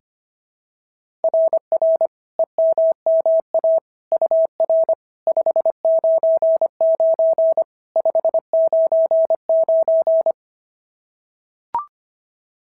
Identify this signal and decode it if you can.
Morse code, heard as a keyed tone, about 22 wpm, 650 Hz. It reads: RR EMMA UR 599 599
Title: SOTA QSO Practice 25wpm